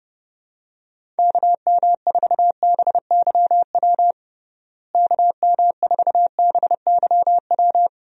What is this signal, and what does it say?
Morse code, heard as a keyed tone, about 30 wpm, 700 Hz. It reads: KM4BYW KM4BYW
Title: USA Call Signs - 6 Characters - Repeated 3x - 30wpm